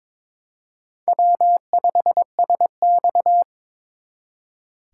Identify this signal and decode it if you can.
Morse code, heard as a keyed tone, about 26 wpm, 700 Hz. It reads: W5SX
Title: Speed Racing - USA 1x2 Call Signs 22wpm